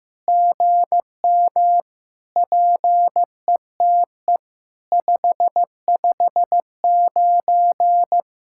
Morse code, heard as a keyed tone, about 14 wpm, 700 Hz. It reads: GM PETE 559
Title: SOTA QSO Practice 15wpm